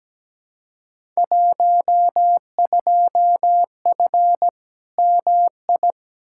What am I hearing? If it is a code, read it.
Morse code, heard as a keyed tone, about 17 wpm, 700 Hz. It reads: 12F MI
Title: Field Day Exchanges Encore - 17wpm